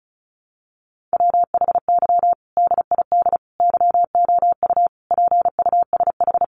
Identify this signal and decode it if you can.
Morse code, heard as a keyed tone, about 34 wpm, 700 Hz. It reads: WHY DID YOU PUSH
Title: Sentences from Top 1300 Words 35wpm